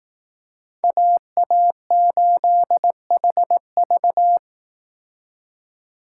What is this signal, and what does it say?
Morse code, heard as a keyed tone, about 18 wpm, 700 Hz. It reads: AA8HV
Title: Speed Racing - USA 2x2 Call Signs 15wpm